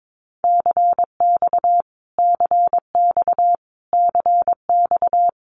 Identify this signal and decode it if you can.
Morse code, heard as a keyed tone, about 22 wpm, 700 Hz. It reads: CX CX CX
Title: Abbreviations Repeated 3x - 22wpm